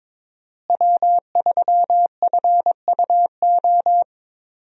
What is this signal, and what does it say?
Morse code, heard as a keyed tone, about 22 wpm, 700 Hz. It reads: W3FUO
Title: USA Call Signs - 5 Characters - Repeated 3x - 22wpm